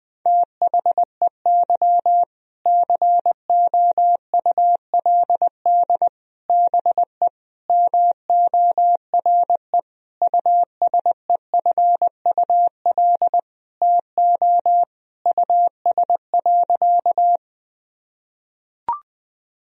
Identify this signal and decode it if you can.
Morse code, heard as a keyed tone, about 21 wpm, 700 Hz. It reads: THEY COULD BE MORE USEFUL TO US.
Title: Sentences from Top 1000 Words - Review Part 4 - 20wpm